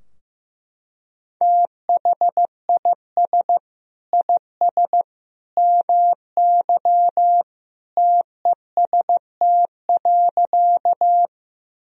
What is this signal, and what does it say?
Morse code, heard as a keyed tone, about 15 wpm, 700 Hz. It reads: THIS IS MY TEST.
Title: Sentences from Top 1000 Words - Review Part 3 - 15wpm